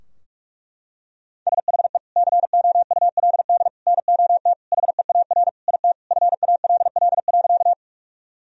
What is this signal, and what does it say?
Morse code, heard as a keyed tone, about 45 wpm, 700 Hz. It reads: SHE COULD NOT HEAR IT FALL.
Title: Sentences from Top 1000 Words - Review Part 2 - 45wpm